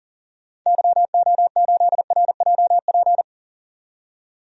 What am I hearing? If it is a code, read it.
Morse code, heard as a keyed tone, about 38 wpm, 700 Hz. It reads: YO8RJP
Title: DXCC-100 Call Signs - 6 Characters - 40wpm